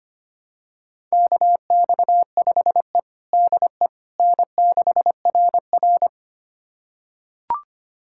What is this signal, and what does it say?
Morse code, heard as a keyed tone, about 25 wpm, 700 Hz. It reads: KX5E DE N6RR
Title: USA Callsign DE Callsign - 4 Characters - 25wpm